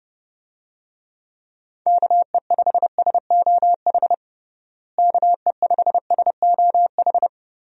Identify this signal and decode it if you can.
Morse code, heard as a keyed tone, about 30 wpm, 700 Hz. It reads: KE5SOH KE5SOH
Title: USA Call Signs - 6 Characters - Repeated 3x - 30wpm